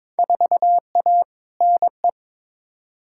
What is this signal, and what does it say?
Morse code, heard as a keyed tone, about 22 wpm, 700 Hz. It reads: 4A NE
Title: Field Day Exchanges - 22wpm